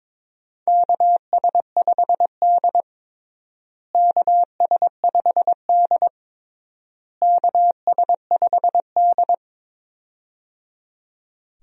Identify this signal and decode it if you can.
Morse code, heard as a keyed tone, about 22 wpm, 700 Hz. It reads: KS5D KS5D KS5D
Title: USA Call Signs - 4 Characters - Repeated 3x - 22wpm